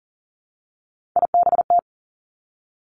Morse code, heard as a keyed tone, about 41 wpm, 700 Hz. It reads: IBT
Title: Three Challenging Letter-Number Combinations 40wpm